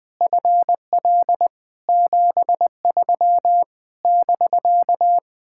Morse code, heard as a keyed tone, about 20 wpm, 700 Hz. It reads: FL 73 <BK>
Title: Speed-Racing Common QSO Phrases Encore - 17wpm